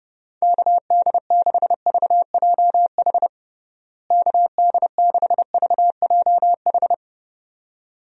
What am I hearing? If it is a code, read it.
Morse code, heard as a keyed tone, about 30 wpm, 700 Hz. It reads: KD6VJH KD6VJH
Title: USA Call Signs - 6 Characters - Repeated 2x - 30wpm